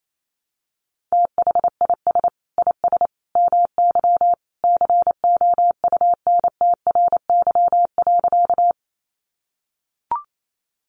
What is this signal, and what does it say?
Morse code, heard as a keyed tone, about 28 wpm, 700 Hz. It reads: THIS IS MY COUNTRY.
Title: Sentences from Top 300 Words 28wpm